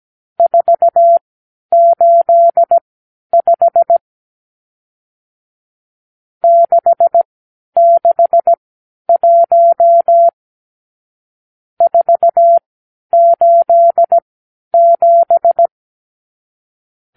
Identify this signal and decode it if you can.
Morse code, heard as a keyed tone, about 17 wpm, spaced as for 10 wpm, 680 Hz. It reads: 485 661 487
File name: k7qo / Lesson 54 (9)